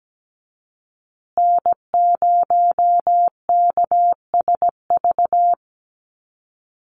Morse code, heard as a keyed tone, about 17 wpm, 700 Hz. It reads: N0KSV
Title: USA Call Signs Repeated 2x 17wpm